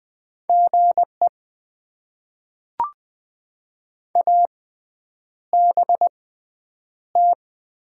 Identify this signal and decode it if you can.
Morse code, heard as a keyed tone, about 20 wpm, spaced as for 9 wpm, 700 Hz. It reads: GE ABT
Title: Top 90 QSO Elements - 6x Character Spacing - 20wpm